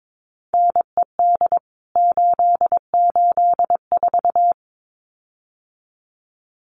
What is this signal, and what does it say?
Morse code, heard as a keyed tone, about 22 wpm, 700 Hz. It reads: NED 884
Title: CWT Exchange 22wpm